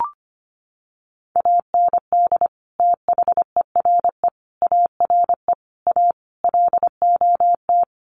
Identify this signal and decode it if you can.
Morse code, heard as a keyed tone, about 25 wpm, 700 Hz. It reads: AND THERE ARE A LOT
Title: Sets of 5 Words Encore2 25wpm